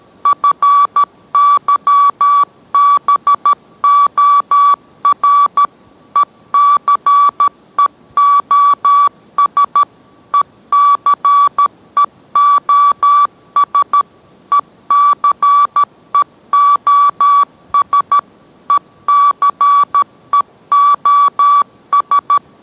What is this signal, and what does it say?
Morse code, heard as a keyed tone, about 13 wpm, 1150 Hz. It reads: FYBOR ECEOS ECEOS ECEOS ECEOS